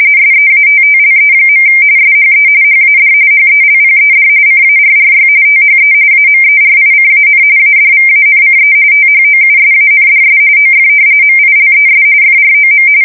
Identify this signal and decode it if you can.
RTTY, about 45.45 baud. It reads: SEATTLE. SHE HAD HELPED CARRY THE AMERICAN ADVANCE THOUSANDS OF MILES FROM
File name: fsk170 / battleshipnj